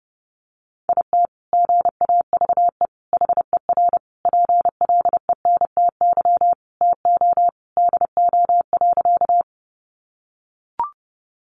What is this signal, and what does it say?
Morse code, heard as a keyed tone, about 30 wpm, 700 Hz. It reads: IT GAVE HER PLENTY TO DO.
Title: Sentences from Top 1300 Words 30wpm